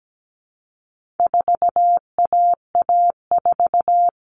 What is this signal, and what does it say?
Morse code, heard as a keyed tone, about 17 wpm, 700 Hz. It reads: 4AA4
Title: All Cut Numbers - 4-Digit 17wpm